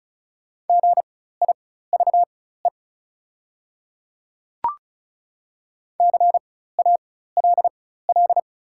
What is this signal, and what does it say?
Morse code, heard as a keyed tone, about 36 wpm, spaced as for 16 wpm, 690 Hz. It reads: GIVE CALL
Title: 4x Character Spacing - Top 200 Words - 35wpm